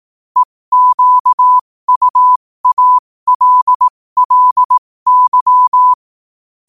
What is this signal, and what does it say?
Morse code, heard as a keyed tone, about 18 wpm, spaced as for 15 wpm, 1000 Hz. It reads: EQUALLY
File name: qotdfiles / 15wpm